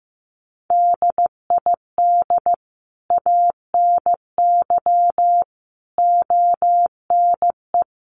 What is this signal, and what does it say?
Morse code, heard as a keyed tone, about 15 wpm, 700 Hz. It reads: DID ANY ONE
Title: Sentences from Top 1000 Words 15wpm